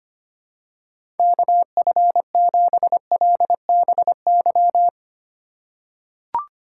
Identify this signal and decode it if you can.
Morse code, heard as a keyed tone, about 25 wpm, 700 Hz. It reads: KF7LBY
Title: USA Call Signs - 4x Character Spacing - 25wpm